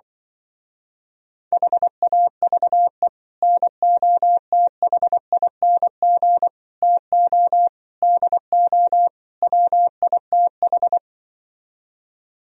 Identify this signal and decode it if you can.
Morse code, heard as a keyed tone, about 24 wpm, 700 Hz. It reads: HAVE NOTHING TO DO WITH
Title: Speed-Racing - Sets of 5 Words 20wpm